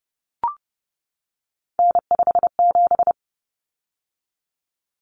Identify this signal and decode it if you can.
Morse code, heard as a keyed tone, about 30 wpm, 700 Hz. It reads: N57
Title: Common Cut Numbers - 3-Digit 30wpm